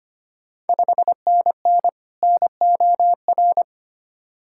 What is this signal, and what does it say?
Morse code, heard as a keyed tone, about 25 wpm, 690 Hz. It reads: 5NN NOR
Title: Ontario QSO Party Exchange 25wpm